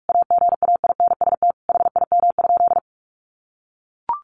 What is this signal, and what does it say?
Morse code, heard as a keyed tone, about 44 wpm, 700 Hz. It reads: AGAINST HIM?
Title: Sentences from Top 1000 Words 45wpm